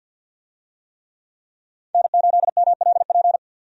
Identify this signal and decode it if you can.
Morse code, heard as a keyed tone, about 52 wpm, 700 Hz. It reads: N8MLP
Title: USA Call Signs - 5 Characters - 50wpm